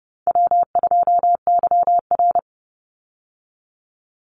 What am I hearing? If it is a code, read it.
Morse code, heard as a keyed tone, about 30 wpm, 700 Hz. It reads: W2YR
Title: Call Signs Repeated 3x - 30wpm